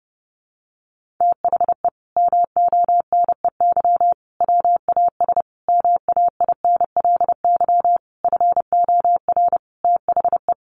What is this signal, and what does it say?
Morse code, heard as a keyed tone, about 30 wpm, 700 Hz. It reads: THE MONEY WAS MAINLY FOR THE
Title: Sentences from Top 1400 Words 30wpm